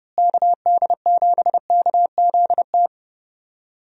Morse code, heard as a keyed tone, about 30 wpm, 700 Hz. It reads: KD7KZT
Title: USA Call Signs - 6 Characters - Repeated 2x - 30wpm